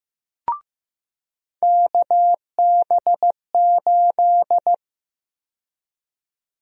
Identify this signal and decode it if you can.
Morse code, heard as a keyed tone, about 16 wpm, 700 Hz. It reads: KB8
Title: USA Call Sign Prefixes - 15wpm